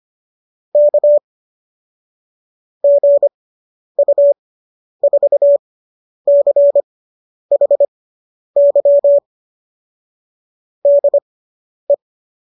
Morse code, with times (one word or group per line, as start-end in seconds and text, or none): 0.8-1.2 K
2.8-9.2 GU4CHY
10.9-11.9 DE